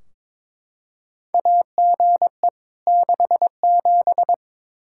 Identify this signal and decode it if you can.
Morse code, heard as a keyed tone, about 22 wpm, 700 Hz. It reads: AGE 67
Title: K1USN SST QSO Practice 22wpm